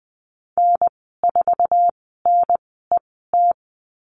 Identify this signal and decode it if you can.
Morse code, heard as a keyed tone, about 20 wpm, spaced as for 14 wpm, 700 Hz. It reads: N4NET
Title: 055 Lesson 10g - Letter r - Call Signs 20wpm